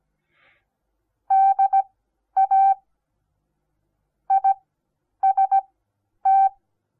Morse code, heard as a keyed tone, about 17 wpm, spaced as for 10 wpm, 780 Hz. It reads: DA IST
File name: Onlinelehrgang / morsen / DJ4UF Morsen08